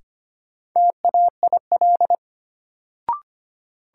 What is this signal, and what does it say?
Morse code, heard as a keyed tone, about 25 wpm, 700 Hz. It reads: TAIL
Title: Sentences from Top 1000 Words - Review Part 3 - New Word Drill - 25wpm